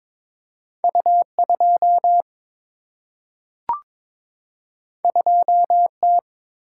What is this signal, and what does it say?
Morse code, heard as a keyed tone, about 22 wpm, 700 Hz. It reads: U 2 2 T
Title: Single-Letter Single-Number 22wpm